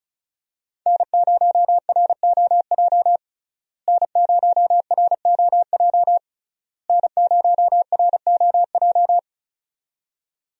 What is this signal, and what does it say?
Morse code, heard as a keyed tone, about 34 wpm, 700 Hz. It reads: N0ROJ N0ROJ N0ROJ
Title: USA Call Signs - 5 Characters - Repeated 3x - 35wpm